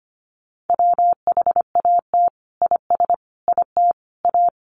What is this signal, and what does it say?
Morse code, heard as a keyed tone, about 25 wpm, 700 Hz. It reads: WHAT IS IT A
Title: Sentences from Top 1300 Words 25wpm